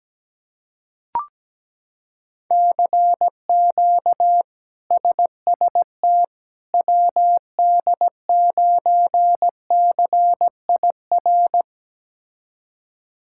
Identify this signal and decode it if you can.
Morse code, heard as a keyed tone, about 17 wpm, 700 Hz. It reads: CQ SST WD9CIR
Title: K1USN SST QSO Practice - 17wpm